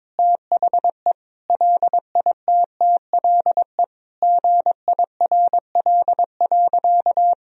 Morse code, heard as a keyed tone, about 22 wpm, 700 Hz. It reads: THE LITTLE GIRL.
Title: Sentences from Top 400 Words and Abbreviations 22wpm